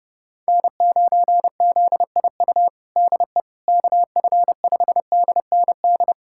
Morse code, heard as a keyed tone, about 32 wpm, 700 Hz. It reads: N9ZIU DE KF5DND
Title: USA Callsign DE Callsign 30wpm